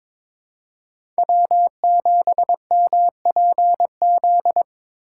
Morse code, heard as a keyed tone, about 22 wpm, 700 Hz. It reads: W7MPZ
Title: USA Call Signs - 5 Characters - 6x Character Spacing - 22wpm